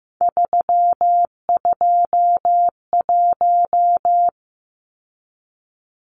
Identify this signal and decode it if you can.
Morse code, heard as a keyed tone, about 15 wpm, 700 Hz. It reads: 321
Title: Numbers - 3-Digit 15wpm